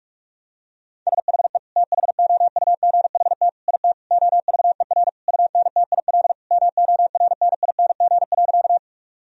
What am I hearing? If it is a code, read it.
Morse code, heard as a keyed tone, about 46 wpm, 700 Hz. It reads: SHE THOUGHT IT OVER UNTIL MORNING.
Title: Sentences from Top 1000 Words - Review Part 1 - 45wpm